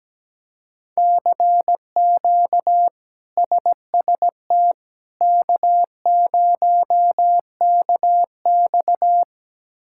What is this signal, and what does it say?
Morse code, heard as a keyed tone, about 17 wpm, 700 Hz. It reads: CQ SST K0KX
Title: K1USN SST QSO Practice - 17wpm